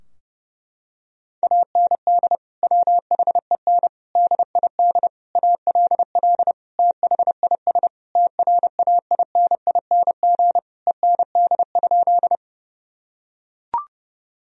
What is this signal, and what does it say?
Morse code, heard as a keyed tone, about 29 wpm, 700 Hz. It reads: AND WHEN DID ALL THIS TRAINING END?
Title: Sentences from Top 1000 Words - Review Part 4 - 30wpm